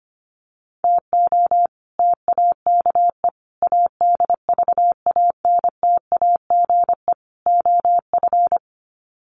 Morse code, heard as a keyed tone, about 25 wpm, 700 Hz. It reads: TO TAKE ADVANTAGE OF
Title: Sets of 4 Words 25wpm